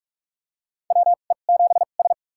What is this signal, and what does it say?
Morse code, heard as a keyed tone, about 46 wpm, spaced as for 33 wpm, 700 Hz. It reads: WE7S
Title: USA Call Signs - 4 Characters - 2x Character Spacing - 45wpm